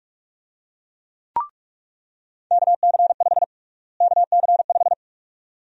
Morse code, heard as a keyed tone, about 46 wpm, 700 Hz. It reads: KC5 KC5
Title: USA Call Sign Prefixes - Repeated 2x - 45wpm